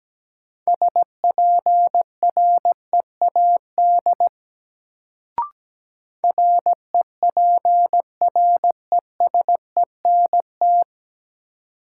Morse code, heard as a keyed tone, about 17 wpm, 700 Hz. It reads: SPREAD REPRESENT
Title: Sentences from Top 1200 Words - New Word Drill - 17wpm